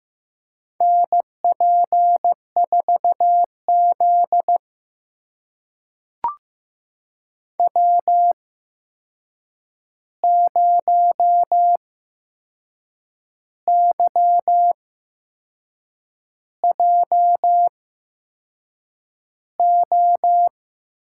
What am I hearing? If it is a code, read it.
Morse code, heard as a keyed tone, about 15 wpm, spaced as for 6 wpm, 700 Hz. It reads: NP4Z W0YJO
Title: USA Call Signs - 8x Character Spacing - 15wpm